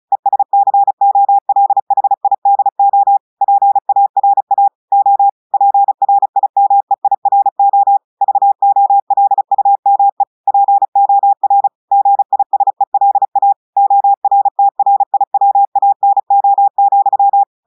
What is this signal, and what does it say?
Morse code, heard as a keyed tone, about 36 wpm, 800 Hz. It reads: ESCOLHIDO PARA O PRIMEIRO VOLUME POR GISELA ORTRIWANO,
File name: audios / portugais / portugais texte en clair 1 vitesse 35 mots minutes